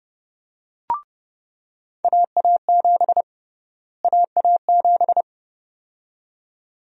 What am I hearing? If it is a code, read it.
Morse code, heard as a keyed tone, about 31 wpm, 700 Hz. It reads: AA7 AA7
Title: USA Call Sign Prefixes - Repeated 2x - 30wpm